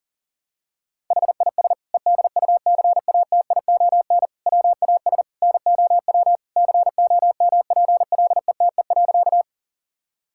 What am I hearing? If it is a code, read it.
Morse code, heard as a keyed tone, about 40 wpm, 700 Hz. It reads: HIS EDUCATION WAS NOW COMPLETE.
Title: Sentences from Top 1000 Words - Review Part 6 - 40wpm